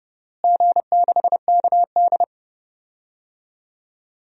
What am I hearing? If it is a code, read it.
Morse code, heard as a keyed tone, about 30 wpm, 700 Hz. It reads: G6KD
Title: DXCC-100 Call Signs - 4 Characters - Repeated 3x - 30wpm